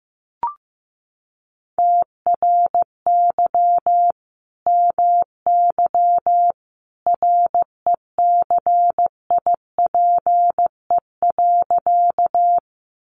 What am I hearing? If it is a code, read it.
Morse code, heard as a keyed tone, about 15 wpm, 700 Hz. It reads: TRY MY RECIPE.
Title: Sentences from Top 1200 Words 15wpm